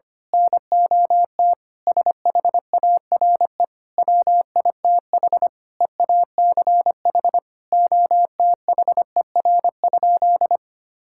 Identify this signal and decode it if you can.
Morse code, heard as a keyed tone, about 25 wpm, 700 Hz. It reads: NOT SHARE WITH EACH OTHER?
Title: Sentences from Top 1000 Words - Review Part 6 - 25wpm